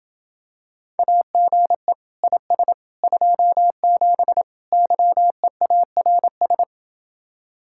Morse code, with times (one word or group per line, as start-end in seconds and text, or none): 1.0-1.9 AGE
2.2-2.7 IS
3.0-4.4 27
4.7-6.6 YEARS